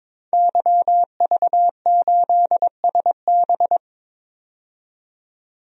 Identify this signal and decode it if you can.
Morse code, heard as a keyed tone, about 22 wpm, 700 Hz. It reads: YV8SB
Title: DXCC-100 Call Signs - 5 Characters - 22wpm